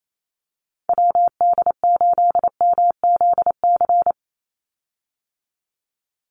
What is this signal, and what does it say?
Morse code, heard as a keyed tone, about 28 wpm, 700 Hz. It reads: WD8MZC
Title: USA Call Signs 28wpm